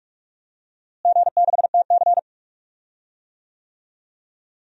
Morse code, heard as a keyed tone, about 41 wpm, 700 Hz. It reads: G6TC
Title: DXCC-100 Call Signs - 4 Characters - Repeated 2x - 45wpm